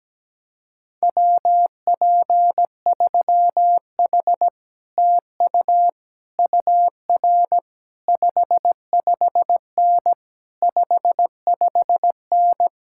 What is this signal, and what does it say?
Morse code, heard as a keyed tone, about 17 wpm, 700 Hz. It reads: WP3H TU UR 55N 55N
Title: Speed-Racing Common QSO Phrases - 17wpm